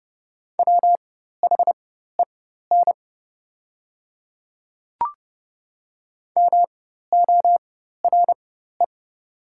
Morse code, heard as a keyed tone, about 30 wpm, spaced as for 14 wpm, 700 Hz. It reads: WHEN MORE
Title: 4x Character Spacing - Top 200 Words - 30wpm